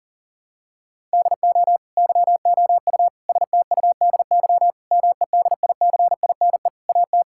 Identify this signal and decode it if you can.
Morse code, heard as a keyed tone, about 40 wpm, 700 Hz. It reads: DO YOU STUDY MEDICINE AT
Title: Sentences from Top 1000 Words - Review Part 6 - 40wpm